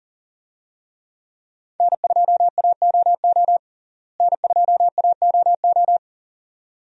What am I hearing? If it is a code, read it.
Morse code, heard as a keyed tone, about 40 wpm, 700 Hz. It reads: N2AOO N2AOO
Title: USA Call Signs - 5 Characters - Repeated 2x - 40wpm